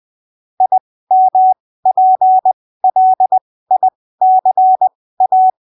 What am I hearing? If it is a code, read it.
Morse code, heard as a keyed tone, about 20 wpm, spaced as for 15 wpm, 750 Hz. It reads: IMPLICA
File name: arquivos / texto / texto 06 20ppm